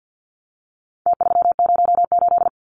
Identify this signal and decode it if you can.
Morse code, heard as a keyed tone, about 50 wpm, 700 Hz. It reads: T308
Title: Numbers - Contest Style - 4-Digit 50wpm